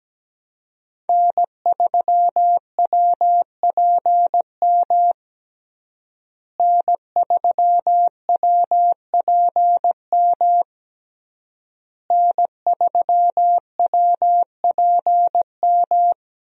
Morse code, heard as a keyed tone, about 17 wpm, 700 Hz. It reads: N3WPM N3WPM N3WPM
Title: USA Call Signs - 5 Characters - Repeated 3x - 17wpm